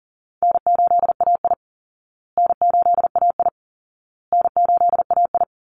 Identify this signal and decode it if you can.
Morse code, heard as a keyed tone, about 40 wpm, 700 Hz. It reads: N8AI N8AI N8AI
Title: Easy Call Signs Repeated 3x - 40wpm